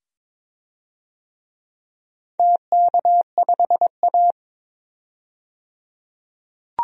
Morse code, heard as a keyed tone, about 22 wpm, 700 Hz. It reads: TK5A
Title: DXCC-100 Call Signs - 4 Characters - 22wpm